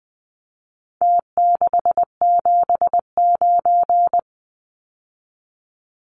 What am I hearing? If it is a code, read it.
Morse code, heard as a keyed tone, about 20 wpm, 700 Hz. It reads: T679
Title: Numbers - Contest Style - 4-Digit 20wpm